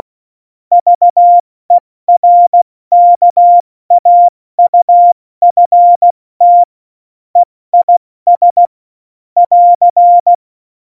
Morse code, heard as a keyed tone, about 16 wpm, 700 Hz. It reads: VERKAUFT EIS <AR>
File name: Morsekurs / Witze / Witz 26. September 17, Tempo 70